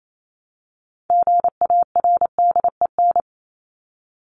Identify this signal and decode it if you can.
Morse code, heard as a keyed tone, about 28 wpm, 700 Hz. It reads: GARDEN
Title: Top 501-750 Words 28wpm